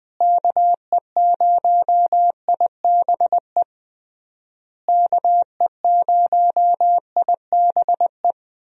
Morse code, heard as a keyed tone, about 20 wpm, 700 Hz. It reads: KE0IBE KE0IBE
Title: USA Call Signs - 6 Characters - Repeated 3x - 20wpm